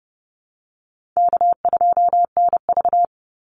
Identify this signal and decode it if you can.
Morse code, heard as a keyed tone, about 30 wpm, 700 Hz. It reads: K2NV
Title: Easy Call Signs - 30wpm